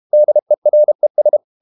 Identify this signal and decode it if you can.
Morse code, heard as a keyed tone, about 32 wpm, 600 Hz. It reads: DERES
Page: Ferie 32 wpm